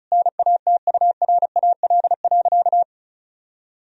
Morse code, heard as a keyed tone, about 36 wpm, 700 Hz. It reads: NATURAL.
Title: Sentences from Top 1000 Words - Review Part 4 - 35wpm